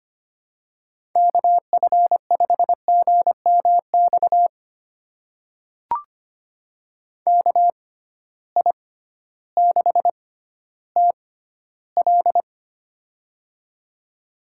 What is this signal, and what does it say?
Morse code, heard as a keyed tone, about 25 wpm, spaced as for 14 wpm, 700 Hz. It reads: KF5GMX KI6TL
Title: USA Call Signs - 6x Character Spacing - 25wpm